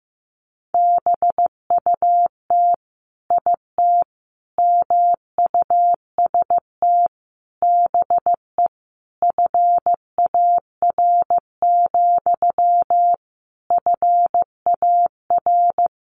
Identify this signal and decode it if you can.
Morse code, heard as a keyed tone, about 14 wpm, 700 Hz. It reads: BUT IT MUST BE FAR, FAR
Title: Sentences from Top 600 Words 15wpm